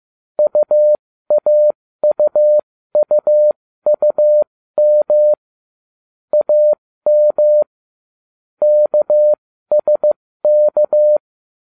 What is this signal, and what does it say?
Morse code, heard as a keyed tone, about 15 wpm, spaced as for 13 wpm, 600 Hz. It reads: UAUUUM AM KSK